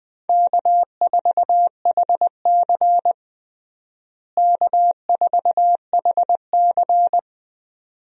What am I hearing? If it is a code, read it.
Morse code, heard as a keyed tone, about 21 wpm, 700 Hz. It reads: K4HC K4HC
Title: USA Call Signs - 4 Characters - Repeated 2x - 20wpm